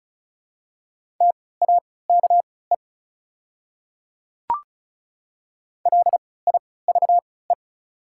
Morse code, heard as a keyed tone, about 36 wpm, spaced as for 20 wpm, 700 Hz. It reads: TAKE LIVE
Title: 3x Character Spacing - Top 300 Words - 35wpm